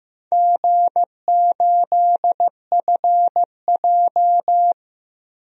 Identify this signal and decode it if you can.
Morse code, heard as a keyed tone, about 15 wpm, 700 Hz. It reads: G8FJ
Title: DXCC-100 Call Signs - 4 Characters - 15wpm